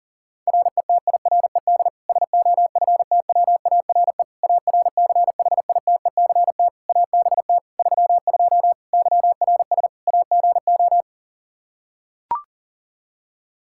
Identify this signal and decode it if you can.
Morse code, heard as a keyed tone, about 40 wpm, 700 Hz. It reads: RETIRED SOFTWARE ARCHITECT ABT 32 YRS AGO
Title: Practice QSO - Exchange 3 - More Info - 40wpm